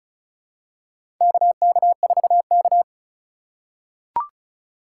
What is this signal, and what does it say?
Morse code, heard as a keyed tone, about 34 wpm, 700 Hz. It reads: KK4K
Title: USA Call Signs - 4x Character Spacing - 35wpm